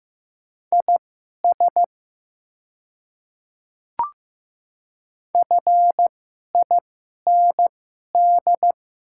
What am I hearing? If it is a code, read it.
Morse code, heard as a keyed tone, about 15 wpm, spaced as for 11 wpm, 700 Hz. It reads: IS FIND
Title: 2x Character Spacing - Top 100 Words - 15wpm